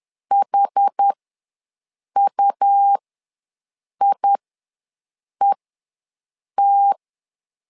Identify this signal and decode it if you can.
Morse code, heard as a keyed tone, about 10 wpm, spaced as for 6 wpm, 800 Hz. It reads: HUIET
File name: LEZIONI / 2 UHDN / 2 UHDN ws 10wpm 2044